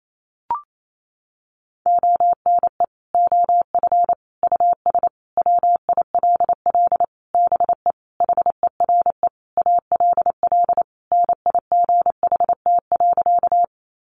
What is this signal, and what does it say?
Morse code, heard as a keyed tone, about 28 wpm, 700 Hz. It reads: ONE OF US WILL BE HERE ALL NIGHT.
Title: Sentences from Top 300 Words 28wpm